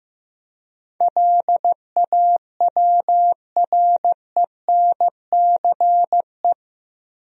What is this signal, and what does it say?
Morse code, heard as a keyed tone, about 15 wpm, 700 Hz. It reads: LAWRENCE
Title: USA Common Names - 15wpm